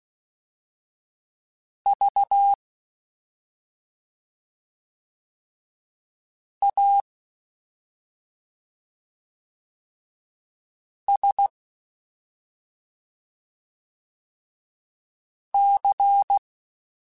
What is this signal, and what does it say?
Morse code, heard as a keyed tone, about 16 wpm, spaced as for 2 wpm, 800 Hz. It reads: VASC
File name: Special INGs-mp3s / Special INGs-mp3s / Slow ing @10